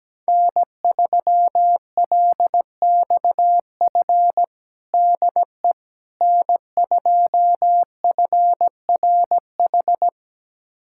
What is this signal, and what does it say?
Morse code, heard as a keyed tone, about 17 wpm, 700 Hz. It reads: N3LXF DE N2FRH
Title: USA Callsign DE Callsign - 5 Characters - 17wpm